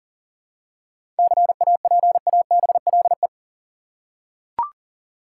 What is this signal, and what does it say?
Morse code, heard as a keyed tone, about 40 wpm, 700 Hz. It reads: CAPABLE
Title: Seven Letter Words - 40wpm